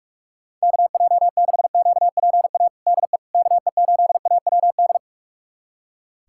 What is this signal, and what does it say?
Morse code, heard as a keyed tone, about 46 wpm, 700 Hz. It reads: KJ6QPA DE KE8AWD
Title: USA Callsign DE Callsign - 6 Characters - 45wpm